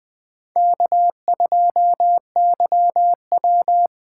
Morse code, heard as a keyed tone, about 20 wpm, 700 Hz. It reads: K2YW
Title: USA Call Signs - 4x Character Spacing - 20wpm